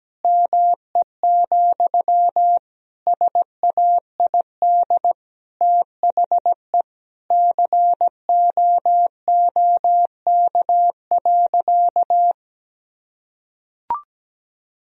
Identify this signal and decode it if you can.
Morse code, heard as a keyed tone, about 17 wpm, 700 Hz. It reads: ME, SAID THE COOK.
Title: Sentences from Top 1000 Words - Review Part 4 - 17wpm